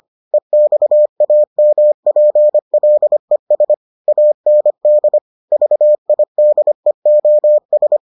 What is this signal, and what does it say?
Morse code, heard as a keyed tone, about 25 wpm, 600 Hz. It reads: EXAMPLES AND VIDEOS